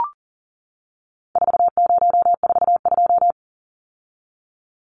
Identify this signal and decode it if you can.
Morse code, heard as a keyed tone, about 40 wpm, 700 Hz. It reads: 4042
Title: Numbers - 4-Digit 40wpm